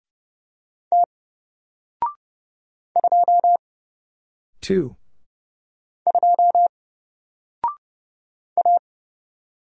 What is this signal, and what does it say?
Morse code, heard as a keyed tone, about 30 wpm, 700 Hz. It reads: T 2 2 A
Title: Single Letter-Number 30wpm